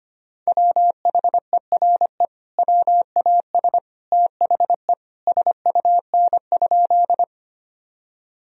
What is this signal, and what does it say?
Morse code, heard as a keyed tone, about 25 wpm, 700 Hz. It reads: WHERE WAS THE SUN?
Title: Sentences from Top 1000 Words - Review Part 1 - 25wpm